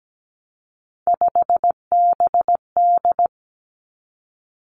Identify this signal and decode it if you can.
Morse code, heard as a keyed tone, about 17 wpm, 700 Hz. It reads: 5BD
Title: Three Challenging Letter-Number Combinations 17wpm